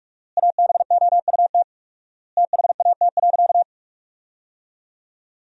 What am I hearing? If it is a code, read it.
Morse code, heard as a keyed tone, about 46 wpm, 700 Hz. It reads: ABOUT THAT.
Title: 4x Word-Spacing - Sentences from Top 400 Words - 45wpm